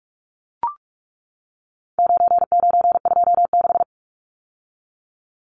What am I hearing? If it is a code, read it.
Morse code, heard as a keyed tone, about 44 wpm, 700 Hz. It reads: 9926
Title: Numbers - 4-Digit 45wpm